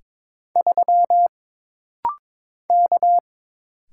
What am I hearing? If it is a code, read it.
Morse code, heard as a keyed tone, about 22 wpm, 700 Hz. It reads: 3 K
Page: This rapid-fire timing is geared to push your ICR (Instant Character Recognition) skills.